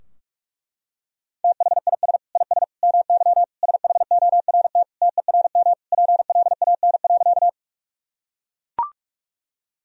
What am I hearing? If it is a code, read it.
Morse code, heard as a keyed tone, about 41 wpm, 700 Hz. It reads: THIS IS MY SHORT TERM PLAN.
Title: Sentences from Top 1000 Words - Review Part 4 - 45wpm